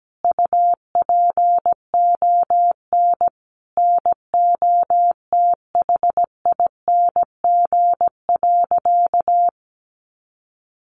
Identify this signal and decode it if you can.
Morse code, heard as a keyed tone, about 17 wpm, 700 Hz. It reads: UPON NOTHING.
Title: Sentences from Top 1200 Words 17wpm